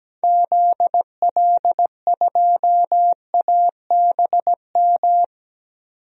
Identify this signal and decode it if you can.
Morse code, heard as a keyed tone, about 17 wpm, 700 Hz. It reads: ZL2ABM
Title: DXCC-100 Call Signs - 6 Characters - 17wpm